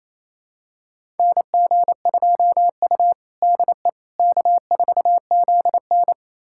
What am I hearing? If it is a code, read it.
Morse code, heard as a keyed tone, about 28 wpm, 700 Hz. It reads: NG2U DE K4ZN
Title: USA Callsign DE Callsign - 4 Characters - 28wpm